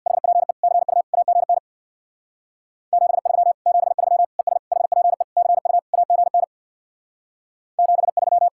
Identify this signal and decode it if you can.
Morse code, heard as a keyed tone, about 64 wpm, 700 Hz. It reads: HPE CU AGN 73 73 ES HPE CU AGN 73